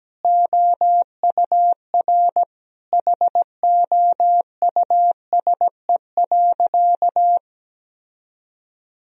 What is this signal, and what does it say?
Morse code, heard as a keyed tone, about 17 wpm, 700 Hz. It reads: OUR HOUSE.
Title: Sentences from Top 1000 Words - Review Part 1 - 17wpm